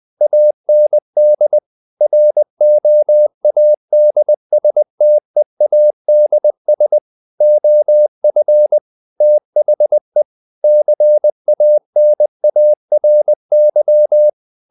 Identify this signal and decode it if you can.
Morse code, heard as a keyed tone, about 20 wpm, 600 Hz. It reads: AND ROADSTEADS OF THE CANARY
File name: the old man and the sea / the old man and the sea part 1 20wpm